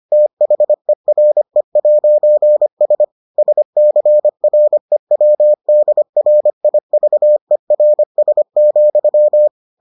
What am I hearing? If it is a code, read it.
Morse code, heard as a keyed tone, about 25 wpm, 600 Hz. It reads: THERE'S SCREWDRIVERS,